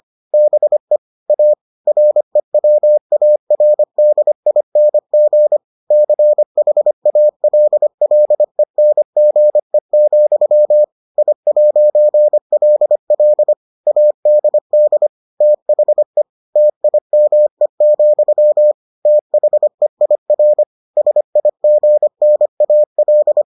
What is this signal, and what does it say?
Morse code, heard as a keyed tone, about 25 wpm, 600 Hz. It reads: BE A REWARDING CHALLENGE, I'LL ADD THE TIME, THEIR SIGNAL